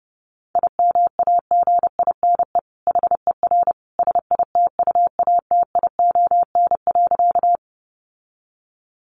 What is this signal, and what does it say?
Morse code, heard as a keyed tone, about 30 wpm, 700 Hz. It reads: IMAGINE HER SITUATION.
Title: Sentences from Top 1400 Words 30wpm